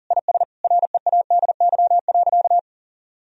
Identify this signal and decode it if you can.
Morse code, heard as a keyed tone, about 40 wpm, 700 Hz. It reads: IS READY.
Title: Sentences from Top 1000 Words - Review Part 1 - 40wpm